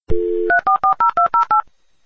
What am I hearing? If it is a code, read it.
DTMF, heard as telephone touch tones, tones of about 120 ms. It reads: 3440208
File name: dialup / dialing